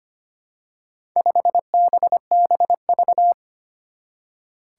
Morse code, heard as a keyed tone, about 25 wpm, 700 Hz. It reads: 5BBV
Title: Four Challenging Letter-Number Combinations 25wpm